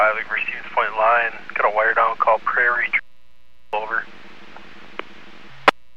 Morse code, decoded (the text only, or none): none